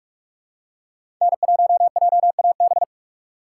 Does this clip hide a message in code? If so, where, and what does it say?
Yes: Morse code from 1.2 to 2.8 s, heard as a keyed tone, about 45 wpm, 700 Hz. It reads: N1JAB